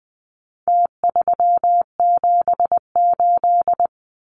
Morse code, heard as a keyed tone, about 20 wpm, 700 Hz. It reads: T378
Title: Numbers - Contest Style - 4-Digit 20wpm